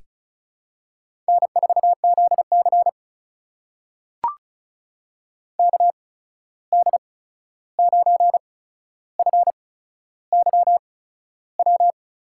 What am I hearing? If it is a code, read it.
Morse code, heard as a keyed tone, about 36 wpm, spaced as for 13 wpm, 700 Hz. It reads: N4ZC KD9FYW
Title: USA Call Signs - 8x Character Spacing - 35wpm